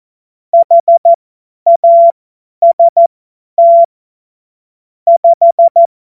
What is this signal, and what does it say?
Morse code, heard as a keyed tone, about 14 wpm, spaced as for 10 wpm, 680 Hz. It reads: HAST 5
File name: Morsekurs / Witze / Witz 03. August 2021, 50 F70